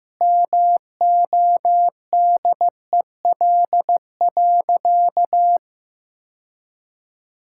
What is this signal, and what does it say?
Morse code, heard as a keyed tone, about 15 wpm, 700 Hz. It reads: MODEL.
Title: Sentences from Top 1000 Words - Review Part 5 - 15wpm